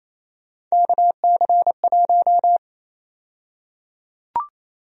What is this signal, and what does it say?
Morse code, heard as a keyed tone, about 28 wpm, 700 Hz. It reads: KC1
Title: USA Call Sign Prefixes - 28wpm